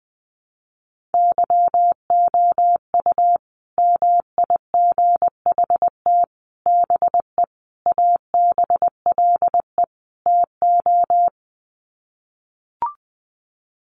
Morse code, heard as a keyed tone, about 20 wpm, 700 Hz. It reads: YOU MIGHT BE ABLE TO
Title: Sets of 5 Words Encore2 20wpm